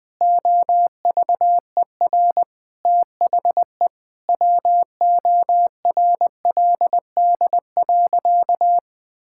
Morse code, heard as a keyed tone, about 20 wpm, 700 Hz. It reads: OVER THE WORLD.
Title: Sentences with 4-Word Phrases - 20wpm